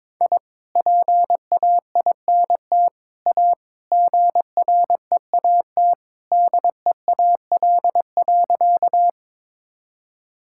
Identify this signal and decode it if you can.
Morse code, heard as a keyed tone, about 22 wpm, 700 Hz. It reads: I PAINT A GREAT DEAL.
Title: Sentences from Top 1000 Words - Review Part 6 - 22wpm